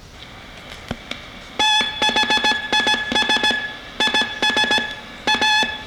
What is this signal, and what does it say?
Morse code, heard as a keyed tone, about 17 wpm, 860 Hz. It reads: THIS IS A